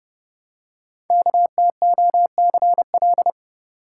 Morse code, heard as a keyed tone, about 28 wpm, 700 Hz. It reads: KTOCL
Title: Code Groups - 5 Letters - 30wpm